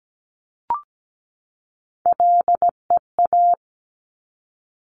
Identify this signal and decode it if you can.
Morse code, heard as a keyed tone, about 17 wpm, 700 Hz. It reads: LEA
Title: Top Three Letter Combinations in Words 17wpm